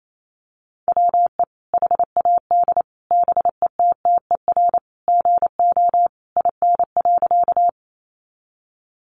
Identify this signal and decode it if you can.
Morse code, heard as a keyed tone, about 28 wpm, 700 Hz. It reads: WE HAD BETTER GO IN.
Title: Sentences from Top 400 Words 28wpm